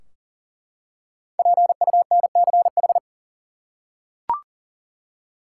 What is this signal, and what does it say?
Morse code, heard as a keyed tone, about 40 wpm, 700 Hz. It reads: PUNCH
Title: Five Letter Words - 40wpm